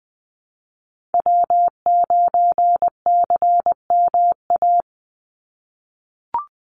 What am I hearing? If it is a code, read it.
Morse code, heard as a keyed tone, about 20 wpm, 700 Hz. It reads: W9CMA
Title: USA Call Signs 20wpm